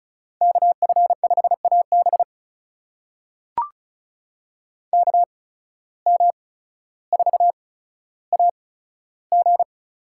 Morse code, heard as a keyed tone, about 34 wpm, spaced as for 15 wpm, 700 Hz. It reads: KF5AB KM4AG
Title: USA Call Signs - 5 Characters - 8x Character Spacing - 35wpm